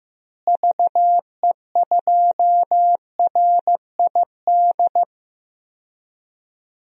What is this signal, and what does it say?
Morse code, heard as a keyed tone, about 15 wpm, 700 Hz. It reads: VE2RID
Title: DXCC-100 Call Signs - 6 Characters - Repeated 3x - 15wpm